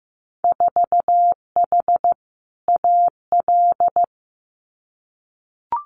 Morse code, heard as a keyed tone, about 15 wpm, 700 Hz. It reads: 4H AL
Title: Winter Field Day Exchanges Encore 15wpm